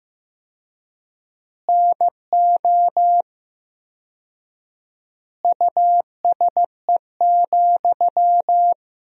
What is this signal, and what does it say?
Morse code, heard as a keyed tone, about 15 wpm, 700 Hz. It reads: NO USE,
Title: 4x Word-Spacing - Sentences from Top 500 Words - 15wpm